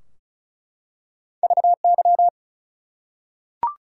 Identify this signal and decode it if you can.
Morse code, heard as a keyed tone, about 36 wpm, 700 Hz. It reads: VY
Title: Top 180 QSO Elements 35wpm